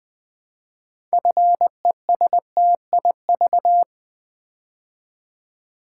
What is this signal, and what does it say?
Morse code, heard as a keyed tone, about 20 wpm, 700 Hz. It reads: FESTIV
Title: Build A Word - 3 Letters At A Time - 8-Letter Words - 20wpm